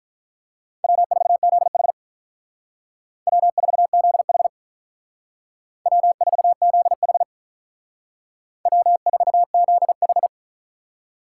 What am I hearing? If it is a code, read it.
Morse code, heard as a keyed tone, about 42 wpm, 700 Hz. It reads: W4ZH W4ZH W4ZH W4ZH
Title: Speed Racing - USA 1x2 Call Signs 35wpm